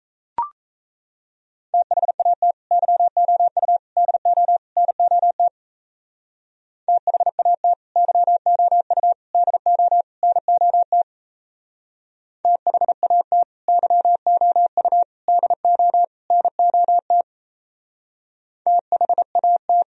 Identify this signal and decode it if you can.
Morse code, heard as a keyed tone, about 35 wpm, 700 Hz. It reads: THAT YOU DO NOT THAT YOU DO NOT THAT YOU DO NOT THAT